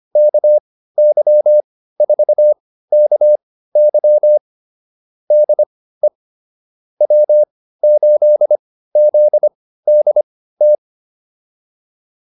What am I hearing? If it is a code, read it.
Morse code, heard as a keyed tone, about 25 wpm, spaced as for 15 wpm, 600 Hz. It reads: KY4KY DE W8ZDT